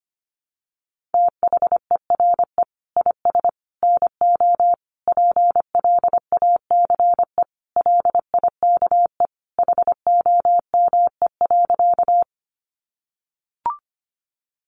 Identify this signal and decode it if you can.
Morse code, heard as a keyed tone, about 25 wpm, 700 Hz. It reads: THERE IS NO PLACE LIKE HOME.
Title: Sentences from Top 200 Words 25wpm